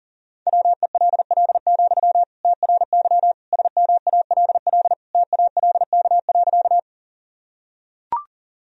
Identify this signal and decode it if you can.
Morse code, heard as a keyed tone, about 40 wpm, 700 Hz. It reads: WELL, TRY SMALL TALK.
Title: Sentences from Top 1000 Words - Review Part 1 - 40wpm